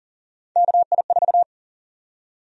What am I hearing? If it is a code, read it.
Morse code, heard as a keyed tone, about 40 wpm, 700 Hz. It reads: KI4
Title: USA Call Sign Prefixes - 40wpm